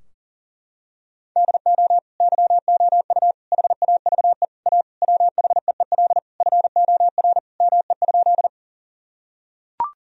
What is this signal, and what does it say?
Morse code, heard as a keyed tone, about 40 wpm, 700 Hz. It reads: DO YOU HAVE A WHEEL FOR ME?
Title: Sentences from Top 1000 Words - Review Part 3 - 40wpm